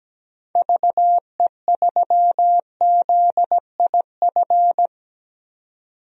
Morse code, heard as a keyed tone, about 17 wpm, 700 Hz. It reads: VE3ZIF
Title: DXCC-100 Call Signs - 6 Characters - 17wpm